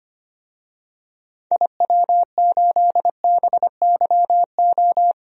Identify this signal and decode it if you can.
Morse code, heard as a keyed tone, about 25 wpm, 700 Hz. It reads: IW8BYO
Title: DXCC-100 Call Signs - 6 Characters - 25wpm